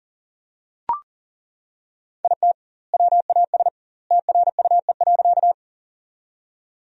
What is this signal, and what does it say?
Morse code, heard as a keyed tone, about 40 wpm, 700 Hz. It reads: IT WAS TRUE.
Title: 2x Word-Spacing - Sentences from Top 400 Words - 40wpm